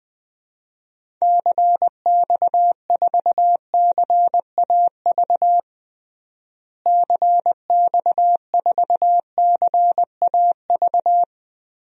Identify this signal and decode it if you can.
Morse code, heard as a keyed tone, about 20 wpm, 700 Hz. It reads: CX4CAV CX4CAV
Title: DXCC-100 Call Signs - 6 Characters - Repeated 3x - 20wpm